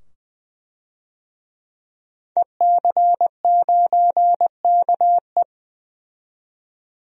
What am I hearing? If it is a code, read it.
Morse code, heard as a keyed tone, about 20 wpm, 700 Hz. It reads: EC9KE
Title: DXCC-100 Call Signs - 5 Characters - 20wpm